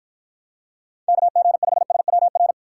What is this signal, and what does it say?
Morse code, heard as a keyed tone, about 53 wpm, 700 Hz. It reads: KG5SWR